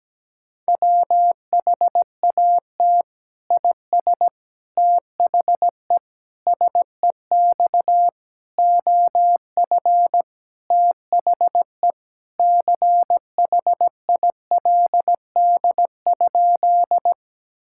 Morse code, heard as a keyed tone, about 17 wpm, 700 Hz. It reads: WHAT IS THE SEX OF THE CHILD?
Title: Sentences from Top 1600 Words 17wpm